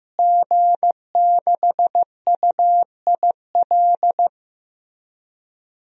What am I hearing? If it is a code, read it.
Morse code, heard as a keyed tone, about 15 wpm, 700 Hz. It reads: G6UIL
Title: DXCC-100 Call Signs - 5 Characters - Repeated 3x - 15wpm